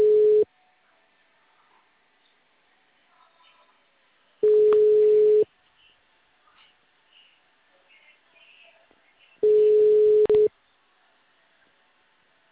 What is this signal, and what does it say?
Ringback tone (Europe, 425 Hz).